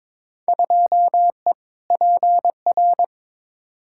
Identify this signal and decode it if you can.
Morse code, heard as a keyed tone, about 21 wpm, 700 Hz. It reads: 2E PR
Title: Field Day Exchanges Encore - 22wpm